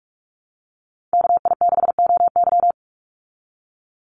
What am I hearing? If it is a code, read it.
Morse code, heard as a keyed tone, about 44 wpm, 700 Hz. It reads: KI6OY
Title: Call Signs Repeated 2x - 45wpm